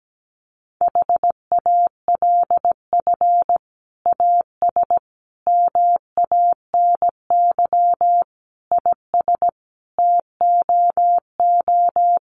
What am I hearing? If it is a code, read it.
Morse code, heard as a keyed tone, about 17 wpm, 700 Hz. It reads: HALF AS MANY IS TOO
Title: Sentences from Top 400 Words 17wpm